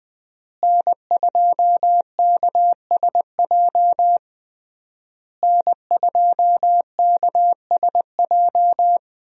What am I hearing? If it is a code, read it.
Morse code, heard as a keyed tone, about 20 wpm, 700 Hz. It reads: N2KSJ N2KSJ
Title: USA Call Signs - 5 Characters - Repeated 3x - 20wpm